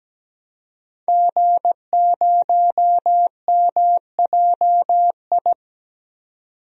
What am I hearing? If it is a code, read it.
Morse code, heard as a keyed tone, about 17 wpm, 700 Hz. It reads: G0MJI
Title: DXCC-100 Call Signs - 5 Characters - 17wpm